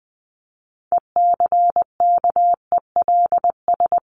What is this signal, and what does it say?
Morse code, heard as a keyed tone, about 20 wpm, 700 Hz. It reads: ECKELS
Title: A Sound Of Thunder 20wpm